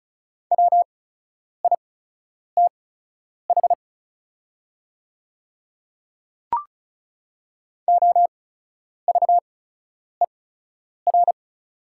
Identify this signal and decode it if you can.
Morse code, heard as a keyed tone, about 36 wpm, spaced as for 10 wpm, 700 Hz. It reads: WITH OVER
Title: 8x Character Spacing - Top 200 Words - 35wpm